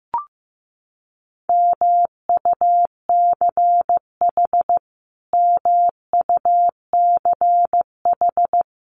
Morse code, heard as a keyed tone, about 15 wpm, 700 Hz. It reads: MUCH MUCH
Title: Top 200 Words Repeated 15wpm